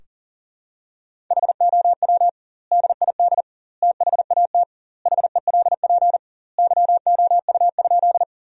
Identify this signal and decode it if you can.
Morse code, heard as a keyed tone, about 40 wpm, 700 Hz. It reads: HOW DID THAT HELP YOU?
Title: 2x Word-Spacing - Sentences from Top 200 Words - 40wpm